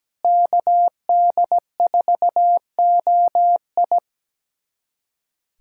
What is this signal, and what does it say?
Morse code, heard as a keyed tone, about 17 wpm, 700 Hz. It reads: KD4OI
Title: USA Callsign DE Callsign 17wpm